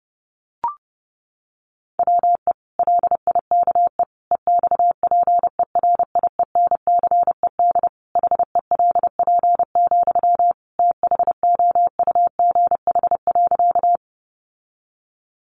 Morse code, heard as a keyed tone, about 30 wpm, 700 Hz. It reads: WE LIKE EXPERIENCED HELP, THOUGH.
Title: Sentences from Top 1100 Words 30wpm